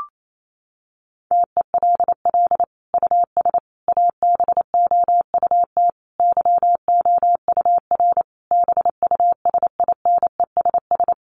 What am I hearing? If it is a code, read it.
Morse code, heard as a keyed tone, about 28 wpm, 700 Hz. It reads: TELL US ABOUT YOUR BUSINESS
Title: Sentences from Top 1200 Words 28wpm